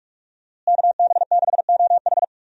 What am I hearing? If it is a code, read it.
Morse code, heard as a keyed tone, about 46 wpm, 700 Hz. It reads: KB6OH
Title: USA Call Signs - 5 Characters - Repeated 2x - 45wpm